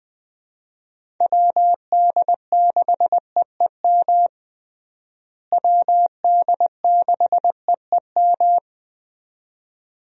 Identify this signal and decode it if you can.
Morse code, heard as a keyed tone, about 20 wpm, 700 Hz. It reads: WD6EEM WD6EEM
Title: USA Call Signs - 6 Characters - Repeated 2x - 20wpm